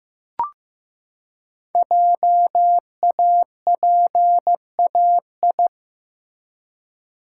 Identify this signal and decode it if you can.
Morse code, heard as a keyed tone, about 15 wpm, 700 Hz. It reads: JAPAI
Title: Code Groups - 5 Letter:Numbers - 15wpm